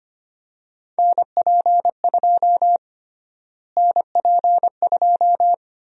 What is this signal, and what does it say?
Morse code, heard as a keyed tone, about 25 wpm, 700 Hz. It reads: NP2 NP2
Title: USA Call Sign Prefixes - Repeated 2x - 25wpm